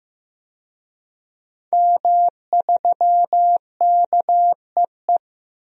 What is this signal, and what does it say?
Morse code, heard as a keyed tone, about 16 wpm, 700 Hz. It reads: M3KEE
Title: DXCC-100 Call Signs - 5 Characters - 15wpm